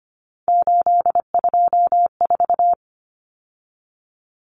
Morse code, heard as a keyed tone, about 25 wpm, 700 Hz. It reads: 824
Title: Numbers - 3-Digit 25wpm